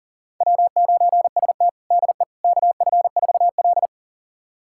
Morse code, heard as a keyed tone, about 40 wpm, 700 Hz. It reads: W9ST DE KF4L
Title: USA Callsign DE Callsign - 4 Characters - 40wpm